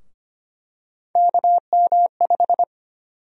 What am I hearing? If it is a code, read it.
Morse code, heard as a keyed tone, about 25 wpm, 700 Hz. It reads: KM5
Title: USA Call Sign Prefixes - Repeated 2x - 25wpm